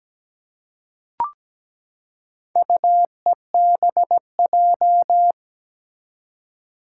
Morse code, heard as a keyed tone, about 17 wpm, 700 Hz. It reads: UEBJ
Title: Four Challenging Letter-Number Combinations 17wpm